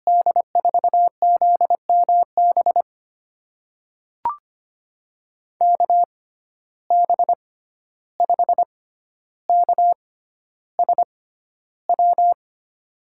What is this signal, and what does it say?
Morse code, heard as a keyed tone, about 25 wpm, spaced as for 12 wpm, 700 Hz. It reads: D4ZMB KB5KSW
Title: USA Call Signs - 6 Characters - 6x Character Spacing - 25wpm